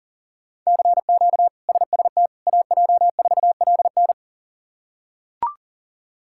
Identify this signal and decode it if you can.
Morse code, heard as a keyed tone, about 40 wpm, 700 Hz. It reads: CQ SST AJ4LN
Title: K1USN SST QSO Practice - 40wpm